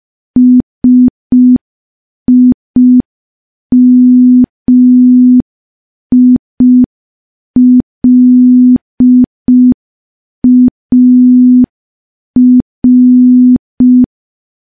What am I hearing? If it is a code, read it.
Morse code, heard as a keyed tone, about 5 wpm, 250 Hz. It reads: SIMILAR
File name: morse / morse-training-05WPM